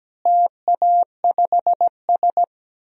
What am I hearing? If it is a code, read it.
Morse code, heard as a keyed tone, about 17 wpm, 700 Hz. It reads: TA5S
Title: DXCC-100 Call Signs - 4 Characters - Repeated 2x - 17wpm